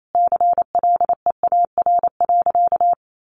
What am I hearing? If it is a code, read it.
Morse code, heard as a keyed tone, about 28 wpm, 700 Hz. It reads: CLEAR.
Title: Sentences from Top 1200 Words 28wpm